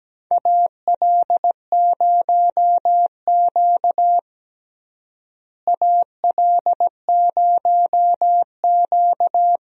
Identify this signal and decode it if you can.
Morse code, heard as a keyed tone, about 17 wpm, 700 Hz. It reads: AL0Q AL0Q
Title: USA Call Signs - 4 Characters - Repeated 3x - 17wpm